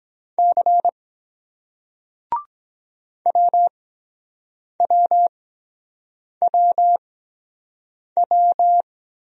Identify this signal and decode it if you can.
Morse code, heard as a keyed tone, about 23 wpm, spaced as for 7 wpm, 700 Hz. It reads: C WWWW